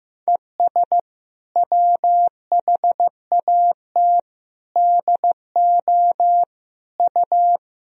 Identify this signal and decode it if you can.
Morse code, heard as a keyed tone, about 16 wpm, 700 Hz. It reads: ES WHAT DO U
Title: Sentences from Top 200 Words and Abbreviations 15wpm